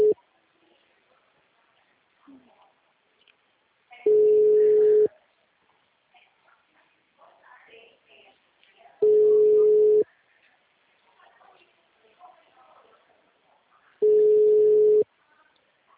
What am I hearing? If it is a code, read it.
Ringback tone (Europe, 425 Hz).
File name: out- -2014- - - .34623